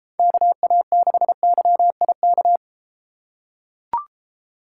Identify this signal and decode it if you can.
Morse code, heard as a keyed tone, about 32 wpm, 700 Hz. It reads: KA6YIK